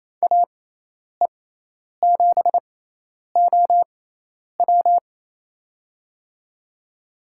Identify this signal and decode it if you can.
Morse code, heard as a keyed tone, about 28 wpm, spaced as for 10 wpm, 700 Hz. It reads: AE7OW
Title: USA Call Signs - 5 Characters - 6x Character Spacing - 28wpm